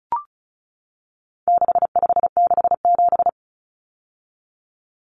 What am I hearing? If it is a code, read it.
Morse code, heard as a keyed tone, about 34 wpm, 700 Hz. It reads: 6567
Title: Numbers - 4-Digit 35wpm